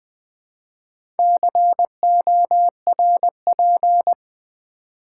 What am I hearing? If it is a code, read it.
Morse code, heard as a keyed tone, about 20 wpm, 700 Hz. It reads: CORP
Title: Build A Word - 4 Letters At A Time - 11-Letter Words - 20wpm